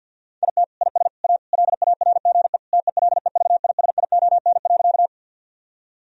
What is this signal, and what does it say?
Morse code, heard as a keyed tone, about 50 wpm, 700 Hz. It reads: IT IS A LARGE TELEVISION.
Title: Sentences from Top 1000 Words - Review Part 4 - 50wpm